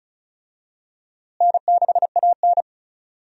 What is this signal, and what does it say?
Morse code, heard as a keyed tone, about 36 wpm, 700 Hz. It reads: N6AN
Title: USA Call Signs - 4 Characters - Repeated 3x - 35wpm